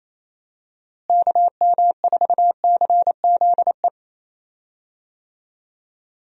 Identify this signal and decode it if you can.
Morse code, heard as a keyed tone, about 28 wpm, 700 Hz. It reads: KM4CZE
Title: USA Call Signs - 6 Characters - Repeated 3x - 28wpm